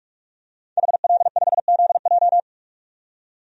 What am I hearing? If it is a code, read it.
Morse code, heard as a keyed tone, about 46 wpm, 700 Hz. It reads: HL5ZJ
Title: DXCC-100 Call Signs - 5 Characters - 45wpm